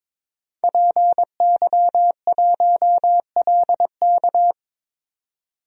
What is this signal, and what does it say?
Morse code, heard as a keyed tone, about 22 wpm, 700 Hz. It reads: PY1LK
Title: DXCC-100 Call Signs - 5 Characters - 22wpm